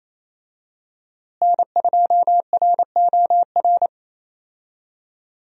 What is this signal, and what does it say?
Morse code, heard as a keyed tone, about 28 wpm, 700 Hz. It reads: N2ROR
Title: USA Call Signs - 5 Characters - Repeated 2x - 28wpm